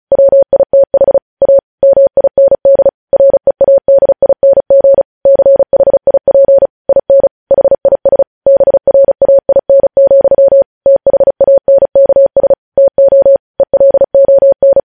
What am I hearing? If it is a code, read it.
Morse code, heard as a keyed tone, about 36 wpm, 550 Hz. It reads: WITH A MIND READING CHIP IN HIS BRAIN, THANKS TO ELON
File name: podcast / News headlines-1x-35x35